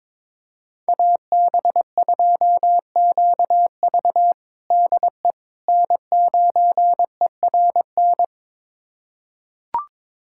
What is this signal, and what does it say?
Morse code, heard as a keyed tone, about 22 wpm, 700 Hz. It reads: AB2QV DE N9ERN
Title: USA Callsign DE Callsign - 5 Characters - 22wpm